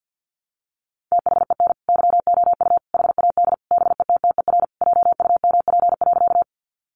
Morse code, heard as a keyed tone, about 50 wpm, 700 Hz. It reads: THEN YOU HAD BETTER JUMP.
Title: Sentences from Top 1300 Words 50wpm